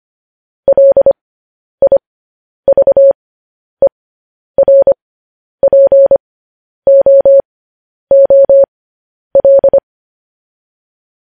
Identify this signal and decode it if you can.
Morse code, heard as a keyed tone, about 25 wpm, spaced as for 10 wpm, 550 Hz. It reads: LIVERPOOL
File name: podcast / News headlines-1x-25x10